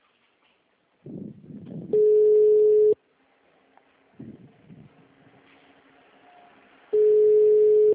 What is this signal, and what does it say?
Ringback tone (Europe, 425 Hz).
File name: out- -2011- - - .20